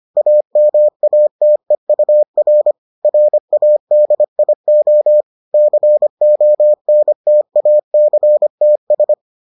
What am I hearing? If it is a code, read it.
Morse code, heard as a keyed tone, about 25 wpm, 600 Hz. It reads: AMATEUR RADIO CONTACTS